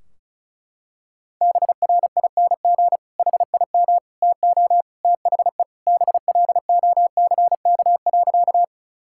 Morse code, heard as a keyed tone, about 34 wpm, 700 Hz. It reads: BRING HIM TO THE BLOCK.
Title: Sentences from Top 1600 Words 35wpm